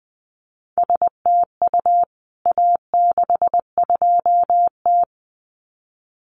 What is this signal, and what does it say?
Morse code, heard as a keyed tone, about 20 wpm, 700 Hz. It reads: STU A62T
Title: CWT Exchange 20wpm